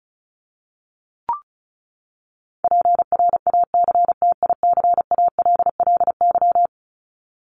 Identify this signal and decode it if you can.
Morse code, heard as a keyed tone, about 34 wpm, 700 Hz. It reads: PRACTICALLY
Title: LY Suffix - Top 100 Words with Base Words 35wpm